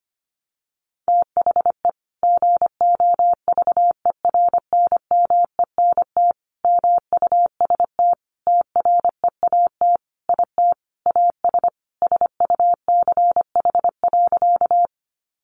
Morse code, heard as a keyed tone, about 25 wpm, 700 Hz. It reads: THE GOVERNMENT MUST TREAT IT AS SUCH.
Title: Sentences from Top 1300 Words 25wpm